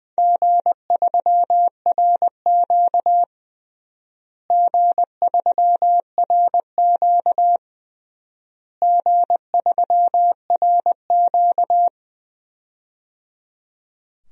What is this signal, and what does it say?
Morse code, heard as a keyed tone, about 21 wpm, 700 Hz. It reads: G3RQ G3RQ G3RQ
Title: DXCC-100 Call Signs - 4 Characters - Repeated 3x - 20wpm